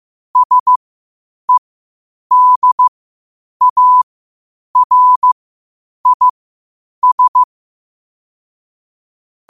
Morse code, heard as a keyed tone, about 15 wpm, spaced as for 8 wpm, 1000 Hz. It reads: SEDARIS
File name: qotdfiles / 8wpm